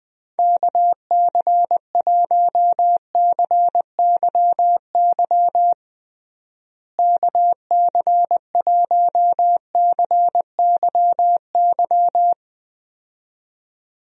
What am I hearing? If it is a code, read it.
Morse code, heard as a keyed tone, about 20 wpm, 700 Hz. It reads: KC1CYY KC1CYY
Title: USA Call Signs - 6 Characters - Repeated 2x - 20wpm